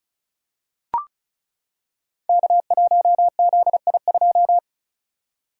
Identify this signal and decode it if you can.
Morse code, heard as a keyed tone, about 36 wpm, 690 Hz. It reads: K1ZI2
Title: Code Groups - 5 Letter:Numbers - 35wpm